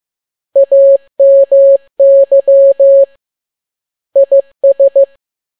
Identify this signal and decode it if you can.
Morse code, heard as a keyed tone, about 15 wpm, 550 Hz. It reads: AMY IS